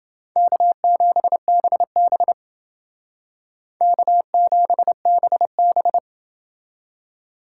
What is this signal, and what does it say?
Morse code, heard as a keyed tone, about 29 wpm, 700 Hz. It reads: K7BB K7BB